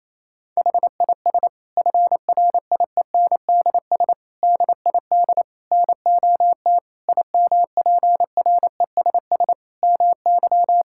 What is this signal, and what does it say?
Morse code, heard as a keyed tone, about 28 wpm, 700 Hz. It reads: HIS FRIENDS DID NOT IMPRESS MY
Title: Sentences from Top 1600 Words 28wpm